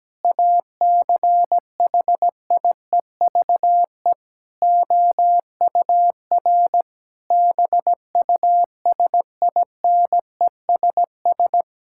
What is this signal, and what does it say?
Morse code, heard as a keyed tone, about 17 wpm, 700 Hz. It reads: ACHIEVE OUR BUSINESS
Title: Sentences from Top 1000 Words - Review Part 6 - 17wpm